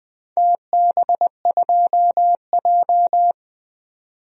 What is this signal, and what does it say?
Morse code, heard as a keyed tone, about 21 wpm, 700 Hz. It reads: TB2J
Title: DXCC-100 Call Signs - 4 Characters - 20wpm